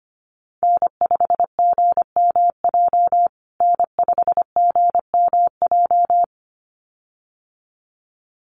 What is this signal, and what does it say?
Morse code, heard as a keyed tone, about 25 wpm, 700 Hz. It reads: N5GMJ N5GMJ
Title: USA Call Signs Repeated 2x 25wpm